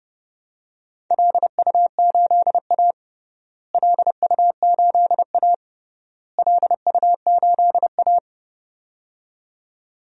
Morse code, heard as a keyed tone, about 30 wpm, 700 Hz. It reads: LU8A LU8A LU8A
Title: DXCC-100 Call Signs - 4 Characters - Repeated 3x - 30wpm